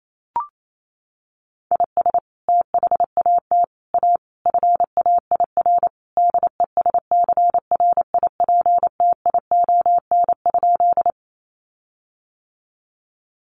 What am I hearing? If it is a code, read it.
Morse code, heard as a keyed tone, about 28 wpm, 700 Hz. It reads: IS THAT A FAIR DESCRIPTION?
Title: Sentences from Top 1300 Words 28wpm